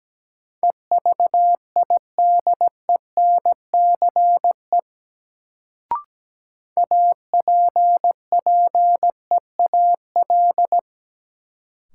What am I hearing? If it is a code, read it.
Morse code, heard as a keyed tone, about 17 wpm, 700 Hz. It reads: EVIDENCE APPEAL
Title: Sentences from Top 1300 Words - New Word Drill - 17wpm